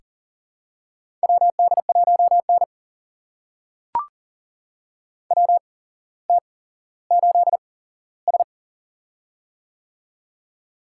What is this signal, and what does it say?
Morse code, heard as a keyed tone, about 40 wpm, spaced as for 17 wpm, 700 Hz. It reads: WD1N WT8S
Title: USA Call Signs - 4 Characters - 8x Character Spacing - 40wpm